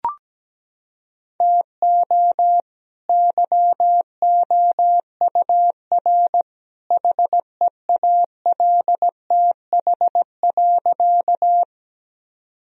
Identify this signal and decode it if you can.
Morse code, heard as a keyed tone, about 17 wpm, 700 Hz. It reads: TO YOUR HEALTH.
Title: Sentences from Top 1000 Words - Review Part 3 - 17wpm